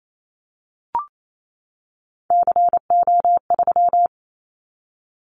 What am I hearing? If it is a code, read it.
Morse code, heard as a keyed tone, about 28 wpm, 700 Hz. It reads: CO3
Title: Callsign Prefixes for Top 100 DXCC - 28wpm